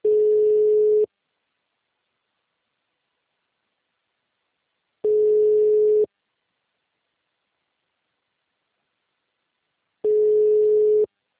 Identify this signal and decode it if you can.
Ringback tone (Europe, 425 Hz).